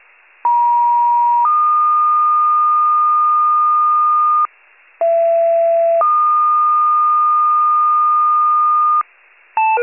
Two-tone page, heard as a one-second tone, then a three-second tone, about 950 then 1220 Hz, about 670 then 1220 Hz. It is heard from 0.4 to 9.0 s.
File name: wcalert